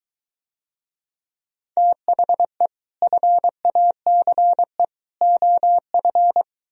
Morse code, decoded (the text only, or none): THE FACE OF